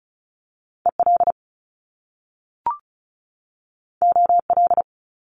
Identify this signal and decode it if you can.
Morse code, heard as a keyed tone, about 36 wpm, 700 Hz. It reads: EL OL
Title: Top Two Letter Combinations in Words 35wpm